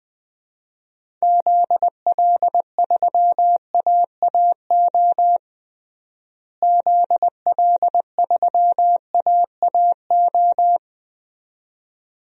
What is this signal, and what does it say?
Morse code, heard as a keyed tone, about 20 wpm, 700 Hz. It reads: ZL3AAO ZL3AAO
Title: DXCC-100 Call Signs - 6 Characters - Repeated 2x - 20wpm